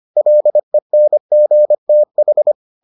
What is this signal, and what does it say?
Morse code, heard as a keyed tone, about 25 wpm, 600 Hz. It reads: LENGTH